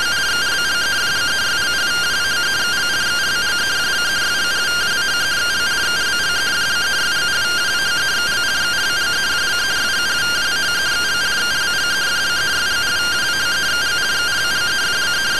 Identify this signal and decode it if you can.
Two-tone teleprinter signal, about 170 Hz shift.